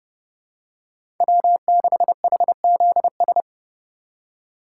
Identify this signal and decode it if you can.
Morse code, heard as a keyed tone, about 30 wpm, 700 Hz. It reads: W6HZS
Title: USA Call Signs - 5 Characters - 3x Character Spacing - 30wpm